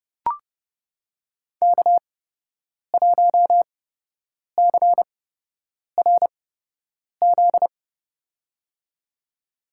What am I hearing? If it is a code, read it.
Morse code, heard as a keyed tone, about 30 wpm, spaced as for 8 wpm, 700 Hz. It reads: K1CRZ
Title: USA Call Signs - 5 Characters - 8x Character Spacing - 30wpm